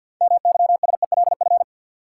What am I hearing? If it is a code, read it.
Morse code, heard as a keyed tone, about 50 wpm, 700 Hz. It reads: MYSELF
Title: Sentences from Top 1300 Words - New Word Drill Alt - 50wpm